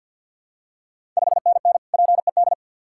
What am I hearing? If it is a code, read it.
Morse code, heard as a keyed tone, about 53 wpm, 700 Hz. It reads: 5NN PED
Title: Ontario QSO Party Exchange 50wpm